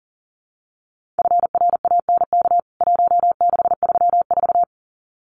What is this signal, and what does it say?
Morse code, heard as a keyed tone, about 40 wpm, 700 Hz. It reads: FRANK 1634
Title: CWT Exchange 40wpm